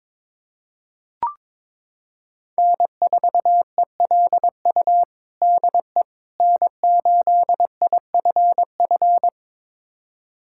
Morse code, heard as a keyed tone, about 21 wpm, 700 Hz. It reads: N4ELU DE N8IFF
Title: USA Callsign DE Callsign - 5 Characters - 22wpm